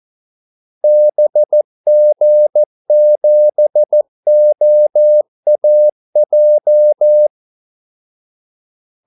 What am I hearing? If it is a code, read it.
Morse code, heard as a keyed tone, about 14 wpm, 600 Hz. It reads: BG7OAJ
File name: Callsigns-14wpm